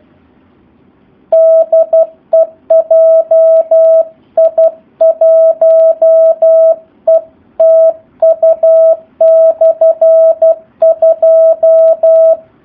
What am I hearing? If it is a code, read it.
Morse code, heard as a keyed tone, about 12 wpm, 650 Hz. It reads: DE JI1ETU/2
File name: jh8cbh / 2019cw / SP04- w